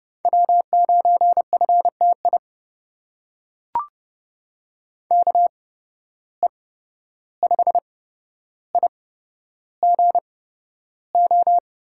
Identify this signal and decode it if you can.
Morse code, heard as a keyed tone, about 30 wpm, spaced as for 12 wpm, 700 Hz. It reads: W9FTI KE5IGO
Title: USA Call Signs - 8x Character Spacing - 30wpm